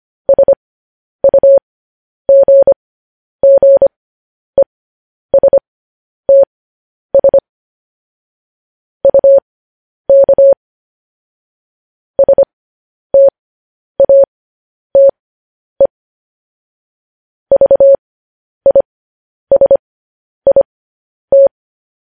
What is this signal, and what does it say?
Morse code, heard as a keyed tone, about 25 wpm, spaced as for 10 wpm, 550 Hz. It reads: SUGGESTS UK STATE VISIT
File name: podcast / News headlines-1x-25x10